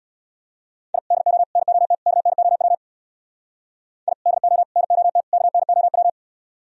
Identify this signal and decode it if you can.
Morse code, heard as a keyed tone, about 62 wpm, 700 Hz. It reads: I DO NOT KNOW I DO NOT KNOW
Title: Speed-Racing - Sets of 4 Words 50wpm